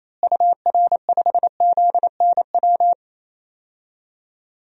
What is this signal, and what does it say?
Morse code, heard as a keyed tone, about 28 wpm, 700 Hz. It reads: UR5ZNW
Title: DXCC-100 Call Signs - 6 Characters - Repeated 3x - 28wpm